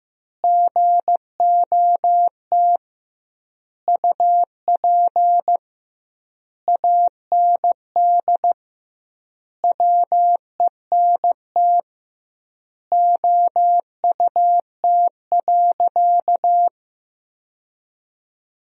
Morse code, heard as a keyed tone, about 15 wpm, 700 Hz. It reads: GOT UP AND WENT OUT.
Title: 2x Word-Spacing - Sentences from Top 300 Words - 15wpm